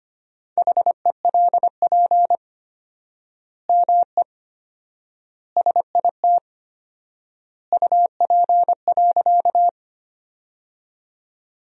Morse code, heard as a keyed tone, about 25 wpm, 700 Hz. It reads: HELP ME SIT UP.
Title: 4x Word-Spacing - Sentences from Top 500 Words - 25wpm